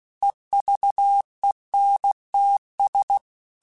Morse code, heard as a keyed tone, about 16 wpm, 800 Hz. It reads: EVENTS
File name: Technical Subjects / Technical Subjects-mp3s / Reactance @15